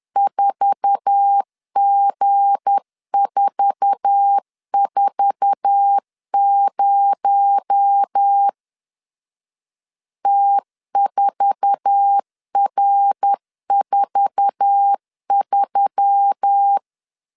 Morse code, heard as a keyed tone, about 10 wpm, 800 Hz. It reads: 4G440 T4R43
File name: LEZIONI / 8 YJ45 / 8 YJ4 10wpm 2633